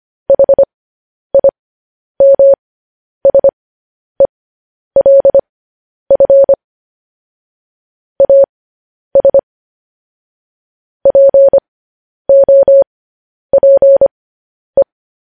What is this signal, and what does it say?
Morse code, heard as a keyed tone, about 25 wpm, spaced as for 10 wpm, 550 Hz. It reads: HIMSELF AS POPE
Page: news-headlines-in-morse-code-at-10-wpm